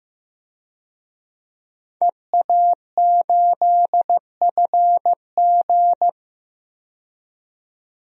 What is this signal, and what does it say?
Morse code, heard as a keyed tone, about 15 wpm, 700 Hz. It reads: EA8FG
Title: DXCC-100 Call Signs - 5 Characters - 15wpm